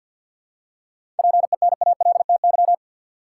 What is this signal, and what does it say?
Morse code, heard as a keyed tone, about 52 wpm, 700 Hz. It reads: PENALTY
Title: Seven Letter Words - 50wpm